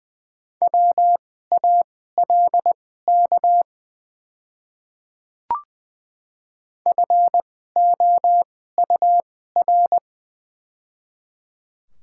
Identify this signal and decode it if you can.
Morse code, heard as a keyed tone, about 20 wpm, spaced as for 15 wpm, 700 Hz. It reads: WALK FOUR
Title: 2x Character Spacing - Top 300 Words - 20wpm